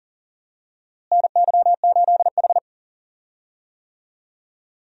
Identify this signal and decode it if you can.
Morse code, heard as a keyed tone, about 40 wpm, 700 Hz. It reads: NY8H
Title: USA Call Signs - 4 Characters - Repeated 3x - 40wpm